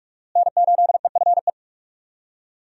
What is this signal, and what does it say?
Morse code, heard as a keyed tone, about 43 wpm, 700 Hz. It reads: N8EFE
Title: USA Call Signs - 5 Characters - Repeated 3x - 45wpm